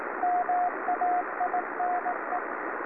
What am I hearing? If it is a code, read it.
Morse code, heard as a keyed tone, about 18 wpm, 720 Hz, with radio noise. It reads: MAINE
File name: beacon / 2011 08-08 by-kb3mxm